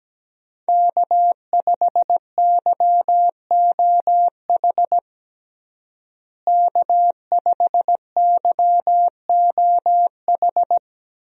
Morse code, heard as a keyed tone, about 17 wpm, 700 Hz. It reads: K5YOH K5YOH
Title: USA Call Signs - 5 Characters - Repeated 2x - 17wpm